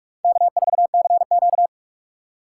Morse code, heard as a keyed tone, about 42 wpm, 700 Hz. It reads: K4CQ
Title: USA Call Signs - 4 Characters - 3x Character Spacing - 45wpm